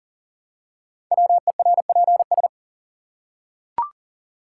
Morse code, heard as a keyed tone, about 42 wpm, 700 Hz. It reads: WERPS
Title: Code Groups - 5 Letters - 40wpm